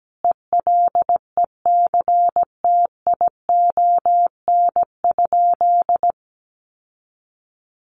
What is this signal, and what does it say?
Morse code, heard as a keyed tone, about 17 wpm, 700 Hz. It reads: ELECTION?
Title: Sentences from Top 1500 Words 17wpm